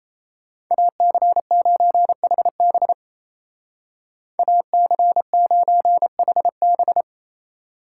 Morse code, heard as a keyed tone, about 35 wpm, 700 Hz. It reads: AC9HB AC9HB
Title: Speed Racing - USA 2x2 Call Signs 28wpm